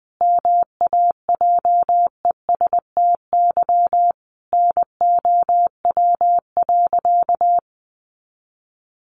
Morse code, heard as a keyed tone, about 20 wpm, 700 Hz. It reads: MAJESTY NOW.
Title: A Sound Of Thunder 20wpm